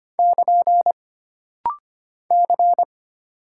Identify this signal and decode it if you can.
Morse code, heard as a keyed tone, about 25 wpm, 700 Hz. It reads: <KN> C
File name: Single Character (Letter:Number:Punctuation:Prosign) - Rapid-Fire - 25wpm